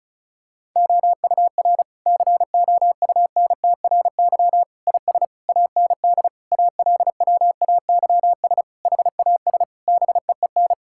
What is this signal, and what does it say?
Morse code, heard as a keyed tone, about 34 wpm, 700 Hz. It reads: OUR COUNTRY IS AND ALWAYS HAS BEEN
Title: Sentences from Top 1000 Words - Review Part 6 - 35wpm